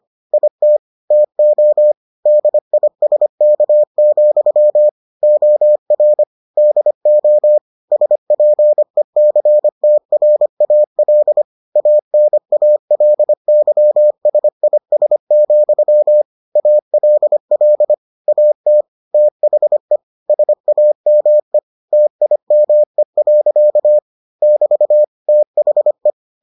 Morse code, heard as a keyed tone, about 25 wpm, 600 Hz. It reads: IT TO DISK, OR DO SPECTRAL ANALYSIS, ALL AT THE SAME TIME. <BT> THE